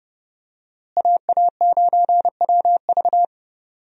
Morse code, heard as a keyed tone, about 30 wpm, 700 Hz. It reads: AA9WV
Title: USA Call Signs - 5 Characters - Repeated 3x - 30wpm